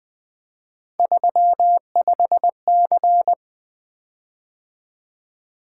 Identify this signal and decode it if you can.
Morse code, heard as a keyed tone, about 20 wpm, 700 Hz. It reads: 35C
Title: Three Letter-Number Combinations - 20wpm